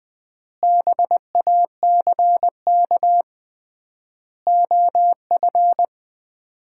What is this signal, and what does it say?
Morse code, heard as a keyed tone, about 20 wpm, 700 Hz. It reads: BACK OF
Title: 3x Word-Spacing - Sets of 4 Words - 20wpm